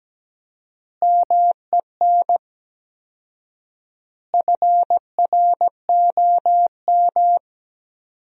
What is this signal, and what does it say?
Morse code, heard as a keyed tone, about 17 wpm, 700 Hz. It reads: MEN FROM
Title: 4x Word-Spacing - Sentences from Top 300 Words - 17wpm